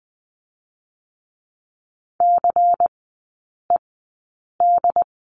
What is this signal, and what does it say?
Morse code, heard as a keyed tone, about 20 wpm, spaced as for 8 wpm, 700 Hz. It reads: CED
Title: 093 Lesson 16e - Letter c - Three Characters 20wpm